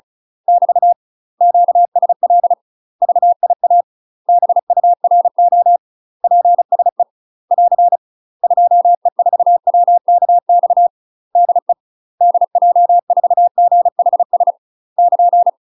Morse code, heard as a keyed tone, about 36 wpm, 700 Hz. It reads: <BT> QSL VIA BURO PSE <AR> 2E4WKX DE DJ4GHS <KN>
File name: UK MASTER QSOs TRAINING / 35WPM / UK MASTER QSOs 0033